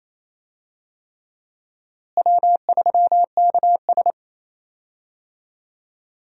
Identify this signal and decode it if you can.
Morse code, heard as a keyed tone, about 28 wpm, 700 Hz. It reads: W3KS
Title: USA Call Signs - 4 Characters - Repeated 3x - 28wpm